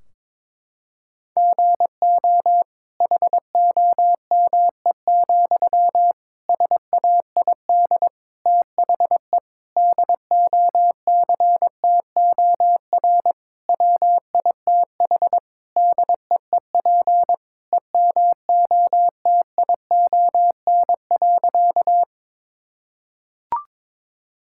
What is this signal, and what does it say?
Morse code, heard as a keyed tone, about 22 wpm, 700 Hz. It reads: GO HOME, SAID THE DOCTOR WITH DEEP EMOTION.
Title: Sentences from Top 1600 Words 22wpm